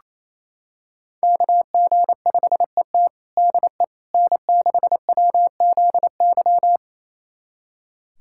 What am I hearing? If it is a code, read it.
Morse code, heard as a keyed tone, about 28 wpm, 700 Hz. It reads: KG5ET DE N6WZY
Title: USA Callsign DE Callsign - 5 Characters - 28wpm